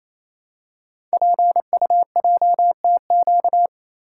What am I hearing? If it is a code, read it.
Morse code, heard as a keyed tone, about 29 wpm, 700 Hz. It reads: PUJTQ
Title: Code Groups - 5 Letter:Numbers - 28wpm